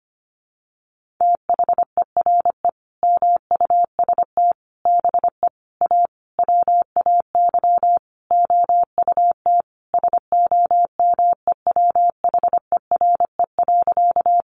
Morse code, heard as a keyed tone, about 25 wpm, 700 Hz. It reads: THERE MUST BE A WAY OUT SOMEWHERE.
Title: Sentences from Top 1400 Words 25wpm